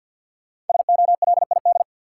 Morse code, heard as a keyed tone, about 53 wpm, 700 Hz. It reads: SOLID
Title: Sentences from Top 1500 Words - New Word Drill Alt - 50wpm